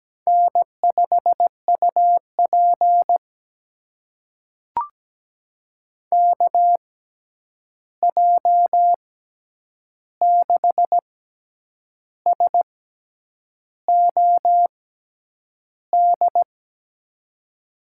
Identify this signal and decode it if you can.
Morse code, heard as a keyed tone, about 17 wpm, spaced as for 8 wpm, 700 Hz. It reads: N5UP KJ6SOD
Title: USA Call Signs - 6x Character Spacing - 17wpm